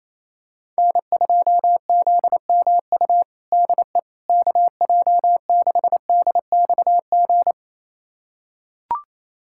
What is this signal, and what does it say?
Morse code, heard as a keyed tone, about 28 wpm, 700 Hz. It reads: N2ZMU DE KJ6DXG
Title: USA Callsign DE Callsign 28wpm